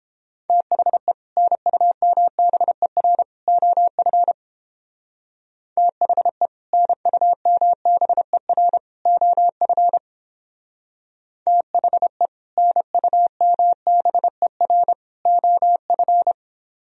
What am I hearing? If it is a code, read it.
Morse code, heard as a keyed tone, about 29 wpm, 700 Hz. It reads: THE NUMBER OF THE NUMBER OF THE NUMBER OF